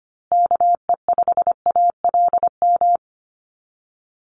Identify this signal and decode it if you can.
Morse code, heard as a keyed tone, about 25 wpm, 700 Hz. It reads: KE5ALM
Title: USA Call Signs Repeated 3x 25wpm